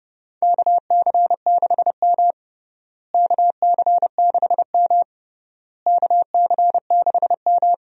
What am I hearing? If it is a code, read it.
Morse code, heard as a keyed tone, about 30 wpm, 700 Hz. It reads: KC6M KC6M KC6M
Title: USA Call Signs - 4 Characters - Repeated 3x - 30wpm